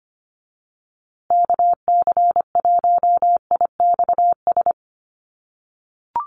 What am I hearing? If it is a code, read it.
Morse code, heard as a keyed tone, about 25 wpm, 700 Hz. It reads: KC1IXS
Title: USA Call Signs Repeated 3x 25wpm